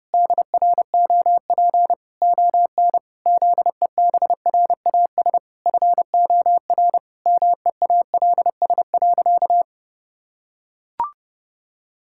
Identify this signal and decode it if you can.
Morse code, heard as a keyed tone, about 30 wpm, 700 Hz. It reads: DROP ON ZEBRAS FOR MEALS.
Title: Pangram Challenge v2 - 30wpm